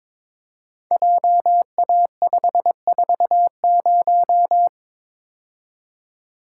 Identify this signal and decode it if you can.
Morse code, heard as a keyed tone, about 22 wpm, 700 Hz. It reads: JA540
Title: Code Groups - 5 Letter:Numbers - 22wpm